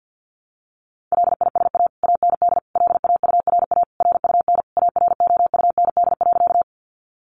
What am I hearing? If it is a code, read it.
Morse code, heard as a keyed tone, about 50 wpm, 700 Hz. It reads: LISA AND LAURA RUN AROUND.
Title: Additional Easy Phrases 50wpm